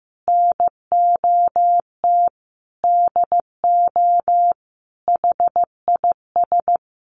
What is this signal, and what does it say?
Morse code, heard as a keyed tone, about 15 wpm, 700 Hz. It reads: NOT DO HIS
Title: Additional Easy Phrases 15wpm